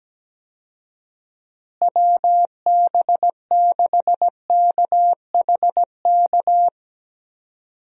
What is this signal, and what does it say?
Morse code, heard as a keyed tone, about 17 wpm, 700 Hz. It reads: WB6KHK
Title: USA Call Signs - 6 Characters - Repeated 3x - 17wpm